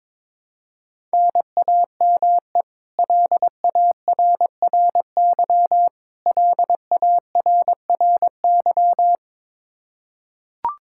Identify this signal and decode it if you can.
Morse code, heard as a keyed tone, about 22 wpm, 700 Hz. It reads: NAME LARRY LARRY
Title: Practice QSO - Exchange 1 - Beginning - 22wpm